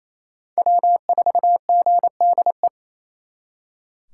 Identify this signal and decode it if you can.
Morse code, heard as a keyed tone, about 37 wpm, 700 Hz. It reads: W4GDE